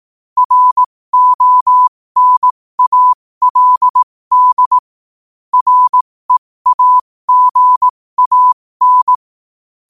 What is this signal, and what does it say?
Morse code, heard as a keyed tone, about 18 wpm, spaced as for 15 wpm, 1000 Hz. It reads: RONALD REAGAN
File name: qotdfiles / 15wpm